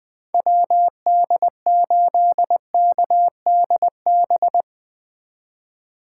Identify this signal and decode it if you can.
Morse code, heard as a keyed tone, about 21 wpm, 700 Hz. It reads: WD8KDB
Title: 2023 SOTA Chaser Call Signs - 20wpm